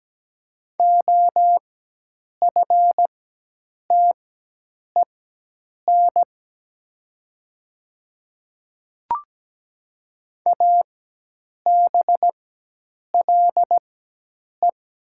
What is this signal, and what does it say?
Morse code, heard as a keyed tone, about 17 wpm, spaced as for 8 wpm, 700 Hz. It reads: OFTEN ABLE
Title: 4x Character Spacing - Top 500 Words - 17wpm